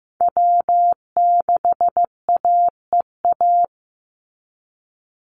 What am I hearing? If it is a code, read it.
Morse code, heard as a keyed tone, about 15 wpm, 700 Hz. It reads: W6AEA
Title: Call Signs Repeated 2x - 15wpm